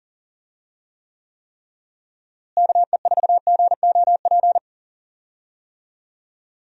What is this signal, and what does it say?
Morse code, heard as a keyed tone, about 40 wpm, 700 Hz. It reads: KE4GOP
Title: USA Call Signs - 6 Characters - Repeated 3x - 40wpm